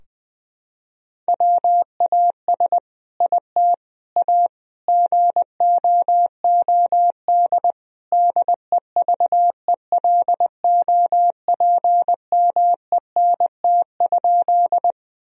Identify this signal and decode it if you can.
Morse code, heard as a keyed tone, about 20 wpm, 700 Hz. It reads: WAS IT A GOOD DEVELOPMENT?
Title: Sentences from Top 1000 Words - Review Part 4 - 20wpm